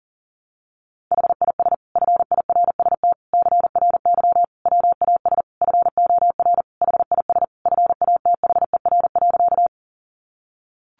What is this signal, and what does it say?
Morse code, heard as a keyed tone, about 40 wpm, 700 Hz. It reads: HIS FIRST CRY WAS FOR HIS FATHER.
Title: Sentences from Top 500 Words 40wpm